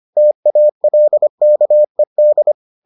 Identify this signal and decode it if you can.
Morse code, heard as a keyed tone, about 25 wpm, 600 Hz. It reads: TALKED